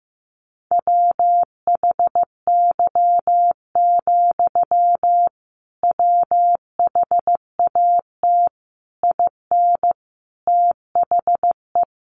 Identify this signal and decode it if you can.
Morse code, heard as a keyed tone, about 15 wpm, 700 Hz. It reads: WHY, WHAT IN THE
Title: Sentences from Top 900 Words 15wpm